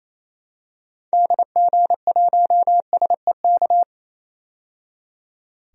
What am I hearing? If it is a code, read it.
Morse code, heard as a keyed tone, about 28 wpm, 700 Hz. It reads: DG1SEK
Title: DXCC-100 Call Signs - 6 Characters - 28wpm